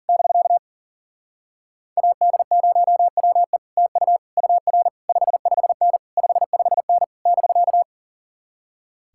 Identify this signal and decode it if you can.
Morse code, heard as a keyed tone, about 52 wpm, 700 Hz. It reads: <BK> AD0WE TU UR 55N 55N <BK>